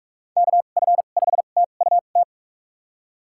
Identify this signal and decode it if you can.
Morse code, heard as a keyed tone, about 46 wpm, spaced as for 32 wpm, 700 Hz. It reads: KF5TUT
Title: USA Call Signs - 6 Characters - 2x Character Spacing - 45wpm